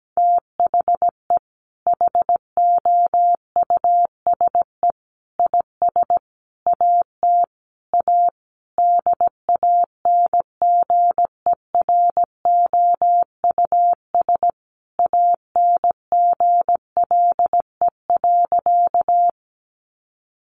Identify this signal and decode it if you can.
Morse code, heard as a keyed tone, about 17 wpm, 700 Hz. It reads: THE HOUSE IS AT A DANGEROUS ANGLE.
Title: Sentences from Top 1500 Words 17wpm